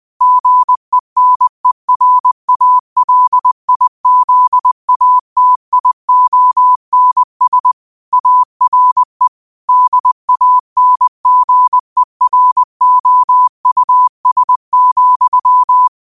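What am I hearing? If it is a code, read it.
Morse code, heard as a keyed tone, about 20 wpm, 1000 Hz. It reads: GENERALIZATIONS ARE DANGEROUS,
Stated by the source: Morse quote of the day sent at 20 wpm.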